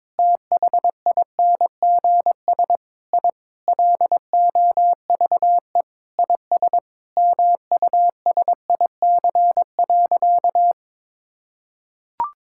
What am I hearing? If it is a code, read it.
Morse code, heard as a keyed tone, about 22 wpm, 700 Hz. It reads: THINGS I LOVE IS MUSIC.
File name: practice / 22wpm / Sentences with 3-Word Phrases - 22wpm